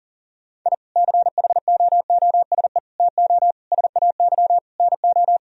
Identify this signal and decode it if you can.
Morse code, heard as a keyed tone, about 37 wpm, 700 Hz. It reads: I CHOOSE TO SAY NO
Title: Sentences from Top 1000 Words - Review Part 4 - 40wpm